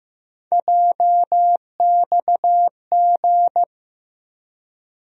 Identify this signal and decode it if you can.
Morse code, heard as a keyed tone, about 15 wpm, 700 Hz. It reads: JXG
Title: USA Call Sign Postfixes - 3 Letters 15wpm